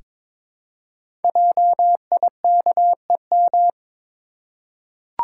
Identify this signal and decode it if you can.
Morse code, heard as a keyed tone, about 22 wpm, 700 Hz. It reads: JIKEM
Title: Code Groups - 5 Letters - 22wpm